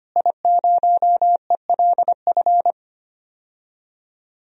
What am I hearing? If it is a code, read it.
Morse code, heard as a keyed tone, about 25 wpm, 700 Hz. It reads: I0ELF
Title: DXCC-100 Call Signs - 5 Characters - Repeated 2x - 25wpm